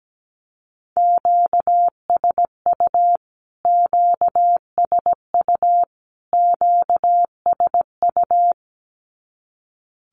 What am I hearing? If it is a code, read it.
Morse code, heard as a keyed tone, about 17 wpm, 700 Hz. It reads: QSU QSU QSU
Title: Q-Codes Repeated 3x 17wpm